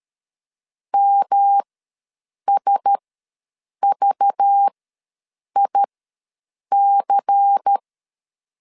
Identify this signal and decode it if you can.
Morse code, heard as a keyed tone, about 13 wpm, spaced as for 7 wpm, 800 Hz. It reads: MSVIC
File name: LEZIONI / 6 PKX0 / 6 P ws 12wpm 3117